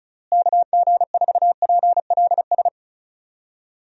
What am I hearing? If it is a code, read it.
Morse code, heard as a keyed tone, about 36 wpm, 700 Hz. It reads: KG4PLS
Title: USA Call Signs - 6 Characters - Repeated 3x - 35wpm